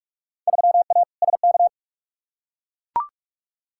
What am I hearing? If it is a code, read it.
Morse code, heard as a keyed tone, about 43 wpm, 700 Hz. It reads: 3A SK
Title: Field Day Exchanges Encore - 45wpm